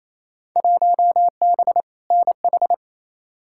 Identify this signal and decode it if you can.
Morse code, heard as a keyed tone, about 28 wpm, 700 Hz. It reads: 1B NH
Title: Field Day Exchanges Encore - 28wpm